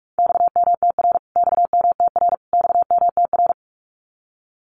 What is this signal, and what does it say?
Morse code, heard as a keyed tone, about 44 wpm, 700 Hz. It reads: XMTR XMTR XMTR
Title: Abbreviations Repeated 3x - 45wpm